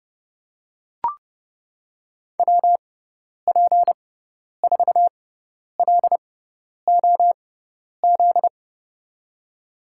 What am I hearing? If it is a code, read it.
Morse code, heard as a keyed tone, about 30 wpm, spaced as for 10 wpm, 700 Hz. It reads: WP4LOZ
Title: USA Call Signs - 6x Character Spacing - 30wpm